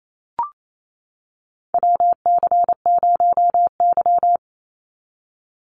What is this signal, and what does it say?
Morse code, heard as a keyed tone, about 28 wpm, 700 Hz. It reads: WC0Y
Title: NA SOTA Chaser Callsigns 28wpm